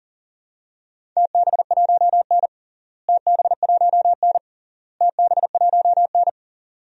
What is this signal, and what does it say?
Morse code, heard as a keyed tone, about 37 wpm, 700 Hz. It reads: TB1N TB1N TB1N
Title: DXCC-100 Call Signs - 4 Characters - Repeated 3x - 40wpm